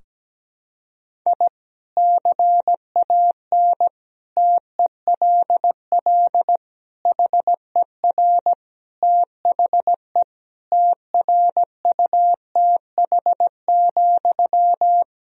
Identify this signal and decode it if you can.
Morse code, heard as a keyed tone, about 17 wpm, 700 Hz. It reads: I CAN TELL HER THE TRUTH,
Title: Sentences from Top 1000 Words - Review Part 3 - 17wpm